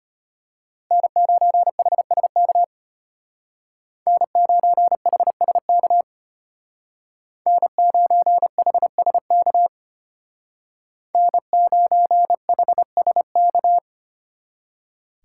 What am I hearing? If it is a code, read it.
Morse code, heard as a keyed tone, about 30 wpm, 700 Hz. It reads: N9HSK N9HSK N9HSK N9HSK
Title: Speed Racing - USA 1x3 Call Signs 25wpm